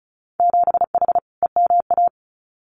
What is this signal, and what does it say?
Morse code, heard as a keyed tone, about 36 wpm, 700 Hz. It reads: 7H EMA
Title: Winter Field Day Exchanges Encore 35wpm